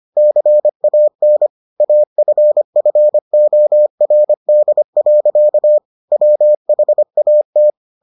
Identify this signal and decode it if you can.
Morse code, heard as a keyed tone, about 25 wpm, 600 Hz. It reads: CAN AFFORD. WHAT